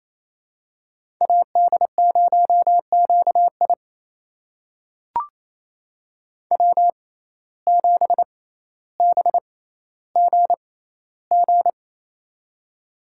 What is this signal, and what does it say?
Morse code, heard as a keyed tone, about 29 wpm, spaced as for 15 wpm, 700 Hz. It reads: AD0QI W7BGG
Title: USA Call Signs - 5 Characters - 6x Character Spacing - 28wpm